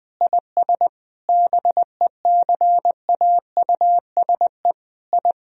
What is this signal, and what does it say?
Morse code, heard as a keyed tone, about 20 wpm, 700 Hz. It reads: IS BECAUSE I
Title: Sentences from Top 1000 Words - Review Part 3 - 20wpm